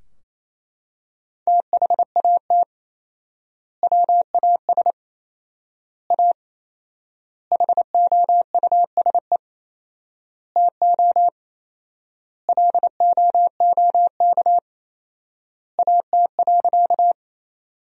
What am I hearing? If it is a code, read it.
Morse code, heard as a keyed tone, about 28 wpm, 700 Hz. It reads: THAT WAS A HOUSE TO LOOK AT.
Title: 4x Word-Spacing - Sentences from Top 200 Words - 28wpm